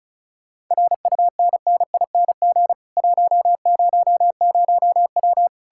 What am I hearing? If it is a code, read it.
Morse code, heard as a keyed tone, about 46 wpm, 700 Hz. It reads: RUNNING 100W